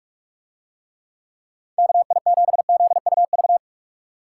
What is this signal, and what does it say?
Morse code, heard as a keyed tone, about 49 wpm, 700 Hz. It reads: KI7ZUV